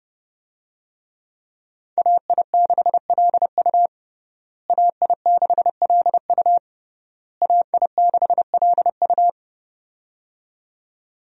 Morse code, heard as a keyed tone, about 30 wpm, 700 Hz. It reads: AI6LU AI6LU AI6LU
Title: USA Call Signs - 5 Characters - Repeated 3x - 30wpm